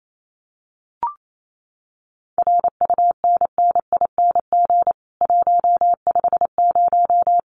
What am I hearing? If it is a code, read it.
Morse code, heard as a keyed tone, about 28 wpm, 700 Hz. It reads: RUNNING 150
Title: Common QSO Phrases 28wpm